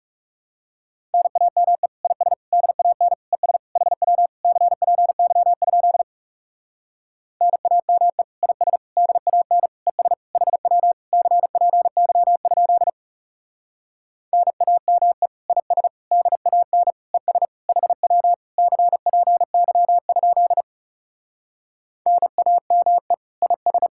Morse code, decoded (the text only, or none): NAME IS DAN ES HW CPY? NAME IS DAN ES HW CPY? NAME IS DAN ES HW CPY? NAME IS